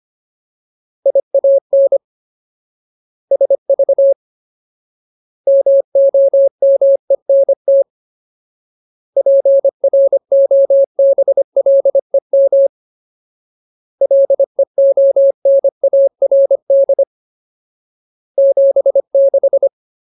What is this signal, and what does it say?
Morse code, heard as a keyed tone, about 25 wpm, 550 Hz. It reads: IAN SV MOMENT PROBLEM LEONARD 76
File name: Ultimate Word List / 25WPM Ultimate